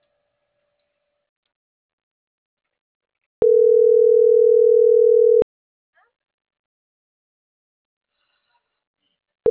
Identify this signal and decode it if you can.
Ringback tone (North America), 440 + 480 Hz.
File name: CENTRAL MAYAB / out vanessa.loyola